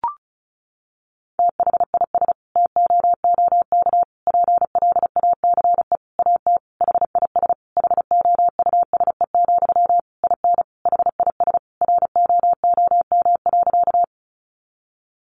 Morse code, heard as a keyed tone, about 34 wpm, 700 Hz. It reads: THIS TOOK PLACE AT HIS HOUSE, IN HIS ROOM.
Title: Sentences from Top 300 Words 35wpm